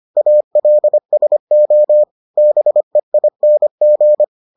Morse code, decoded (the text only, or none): ALSO BEING